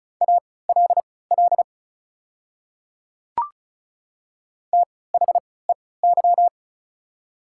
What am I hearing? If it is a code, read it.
Morse code, heard as a keyed tone, about 36 wpm, spaced as for 20 wpm, 700 Hz. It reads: ALL THEY
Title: 3x Character Spacing - Top 200 Words - 35wpm